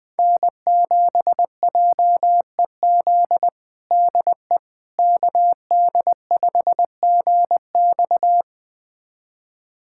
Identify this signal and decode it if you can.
Morse code, heard as a keyed tone, about 20 wpm, 700 Hz. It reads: N7JEZ DE KD5GX
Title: USA Callsign DE Callsign - 5 Characters - 20wpm